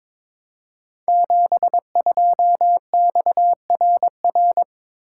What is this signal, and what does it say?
Morse code, heard as a keyed tone, about 22 wpm, 700 Hz. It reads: 72XRR
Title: Code Groups - 5 Letter:Numbers - 22wpm